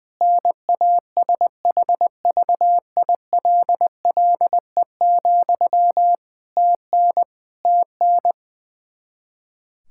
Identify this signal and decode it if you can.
Morse code, heard as a keyed tone, about 20 wpm, 700 Hz. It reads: NASHVILLE, TN TN
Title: Practice QSO - Exchange 1 - Beginning - 20wpm